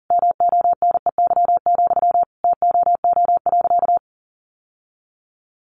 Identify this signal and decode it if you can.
Morse code, heard as a keyed tone, about 40 wpm, 700 Hz. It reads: MONEY, TOO.
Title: Sentences from Top 900 Words 40wpm